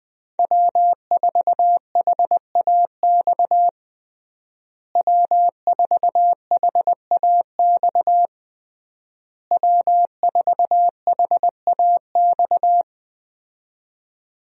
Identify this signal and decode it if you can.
Morse code, heard as a keyed tone, about 21 wpm, 700 Hz. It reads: W4HAX W4HAX W4HAX
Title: USA Call Signs - 5 Characters - Repeated 3x - 20wpm